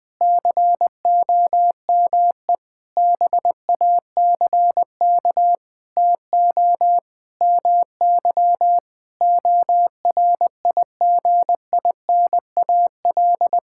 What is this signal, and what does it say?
Morse code, heard as a keyed tone, about 20 wpm, 700 Hz. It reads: COME BACK TO MY ORIGINAL
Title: Sentences from Top 1000 Words - Review Part 6 - 20wpm